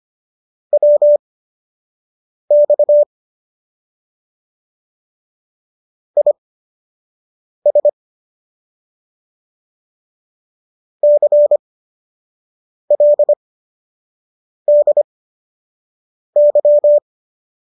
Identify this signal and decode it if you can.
Morse code, heard as a keyed tone, about 25 wpm, spaced as for 6 wpm, 590 Hz. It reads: WX IS CLDY